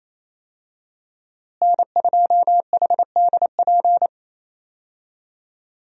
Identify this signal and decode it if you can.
Morse code, heard as a keyed tone, about 28 wpm, 700 Hz. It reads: N2HDP
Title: USA Call Signs - 5 Characters - Repeated 2x - 28wpm